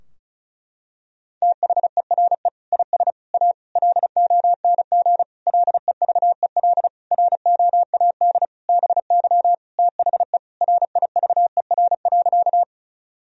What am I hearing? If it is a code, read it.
Morse code, heard as a keyed tone, about 36 wpm, 700 Hz. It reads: THERE IS A LONG LEVEL ROAD BY THE RIVER.
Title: Sentences from Top 1000 Words - Review Part 4 - 35wpm